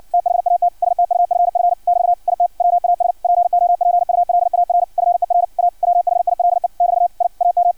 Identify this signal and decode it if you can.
Morse code, heard as a keyed tone, about 56 wpm, 700 Hz. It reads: THAT STUFF <BT> IT WAS POPULAR FER A WHILE <BT> I AM
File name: archived / CRSnet